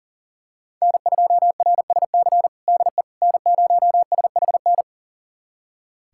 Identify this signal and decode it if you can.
Morse code, heard as a keyed tone, about 40 wpm, 700 Hz. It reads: N2RSC DE N0SHN
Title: USA Callsign DE Callsign - 5 Characters - 40wpm